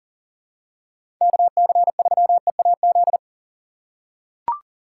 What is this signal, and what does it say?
Morse code, heard as a keyed tone, about 40 wpm, 700 Hz. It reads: KC3EAZ
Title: USA Call Signs - 6 Characters - 4x Character Spacing - 40wpm